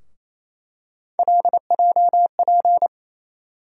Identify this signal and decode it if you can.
Morse code, heard as a keyed tone, about 28 wpm, 700 Hz. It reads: LJP
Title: USA Call Sign Postfixes - 3 Letters 28wpm